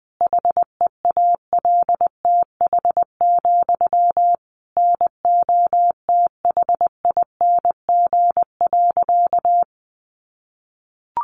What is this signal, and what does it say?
Morse code, heard as a keyed tone, about 20 wpm, 700 Hz. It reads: HEALTH, NOTHING.
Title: Sentences from Top 900 Words 20wpm